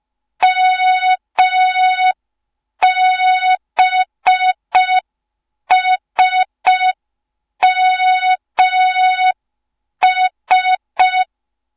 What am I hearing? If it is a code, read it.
Morse code, heard as a keyed tone, about 5 wpm, 3700 Hz. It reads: MBSMS
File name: Morze-HA2MN / hang / 07 GYAKORLAT